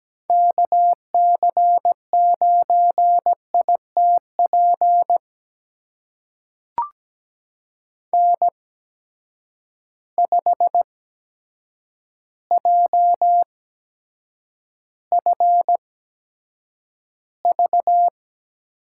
Morse code, heard as a keyed tone, about 17 wpm, spaced as for 8 wpm, 700 Hz. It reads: KC9ITP N5JFV
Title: USA Call Signs - 8x Character Spacing - 17wpm